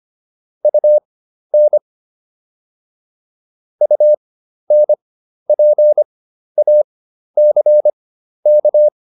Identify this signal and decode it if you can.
Morse code, heard as a keyed tone, about 25 wpm, spaced as for 12 wpm, 600 Hz. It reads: UN UNPACK
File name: UN6 12